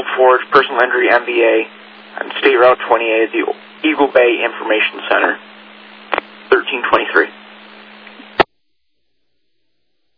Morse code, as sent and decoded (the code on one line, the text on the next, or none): none